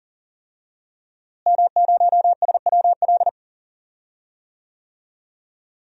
Morse code, heard as a keyed tone, about 39 wpm, 700 Hz. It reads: M0SWL
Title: DXCC-100 Call Signs - 5 Characters - Repeated 3x - 40wpm